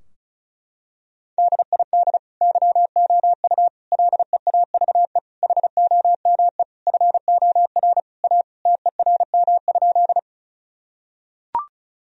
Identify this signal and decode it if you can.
Morse code, heard as a keyed tone, about 36 wpm, 700 Hz. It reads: DID YOU LEAVE HOME FOR A TERM?
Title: Sentences from Top 1000 Words - Review Part 4 - 35wpm